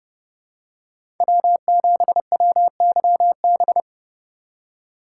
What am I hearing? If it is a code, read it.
Morse code, heard as a keyed tone, about 32 wpm, 700 Hz. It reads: W7WYB
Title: USA Call Signs - 2x Character Spacing - 30wpm